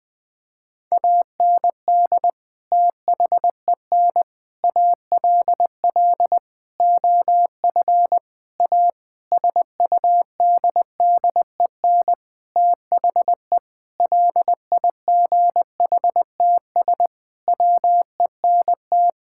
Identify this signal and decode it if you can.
Morse code, heard as a keyed tone, about 20 wpm, 700 Hz. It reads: AND THEN ALL OF A SUDDEN THE LIGHTS WENT
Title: Sentences with 6-Word Phrases - Spoken Beginning - 20wpm